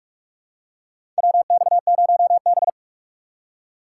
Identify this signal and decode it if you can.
Morse code, heard as a keyed tone, about 45 wpm, 700 Hz. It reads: WX0B
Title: USA Call Signs - 4 Characters - 4x Character Spacing - 45wpm